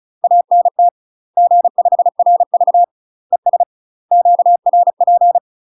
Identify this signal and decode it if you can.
Morse code, heard as a keyed tone, about 37 wpm, 700 Hz. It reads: ANT G5RV ES QRP
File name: UK MASTER QSOs TRAINING / 35WPM / UK MASTER QSOs 0003